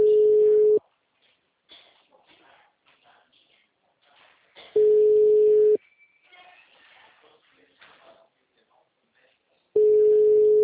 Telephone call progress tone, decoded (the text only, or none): ringback tone